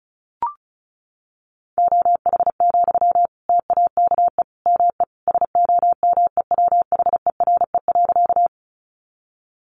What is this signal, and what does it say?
Morse code, heard as a keyed tone, about 34 wpm, 700 Hz. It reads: OH, TAKE ME SOMEWHERE.
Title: Sentences from Top 1400 Words 35wpm